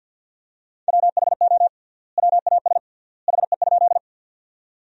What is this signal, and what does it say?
Morse code, heard as a keyed tone, about 50 wpm, 700 Hz. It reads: WHO WAS HE?
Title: 3x Word-Spacing - Sentences from Top 100 Words - 50wpm